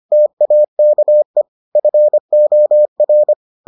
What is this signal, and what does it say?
Morse code, heard as a keyed tone, about 25 wpm, 600 Hz. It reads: TAKE FOR